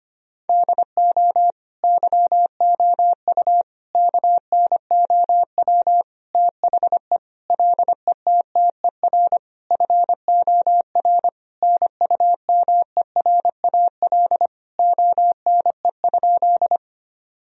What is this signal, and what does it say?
Morse code, heard as a keyed tone, about 25 wpm, 700 Hz. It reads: DO YOU KNOW THE LETTER FOR NUMERAL ONE?
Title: Sentences from Top 1000 Words - Review Part 2 - 25wpm